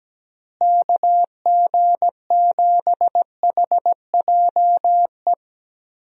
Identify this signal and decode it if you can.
Morse code, heard as a keyed tone, about 17 wpm, 700 Hz. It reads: KG7HJE
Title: USA Call Signs - 6 Characters - Repeated 3x - 17wpm